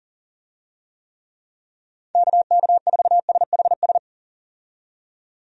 Morse code, heard as a keyed tone, about 40 wpm, 700 Hz. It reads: KK4SHS
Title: USA Call Signs - 6 Characters - 40wpm